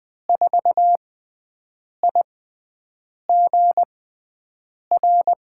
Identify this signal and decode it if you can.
Morse code, heard as a keyed tone, about 20 wpm, spaced as for 7 wpm, 700 Hz. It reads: 4IGR
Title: USA Call Signs - 6 Characters - 6x Character Spacing - 20wpm